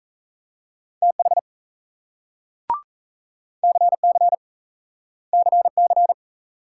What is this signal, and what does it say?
Morse code, heard as a keyed tone, about 38 wpm, 700 Hz. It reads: TH CC CC